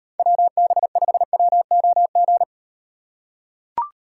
Morse code, heard as a keyed tone, about 38 wpm, 700 Hz. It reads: WB5WOG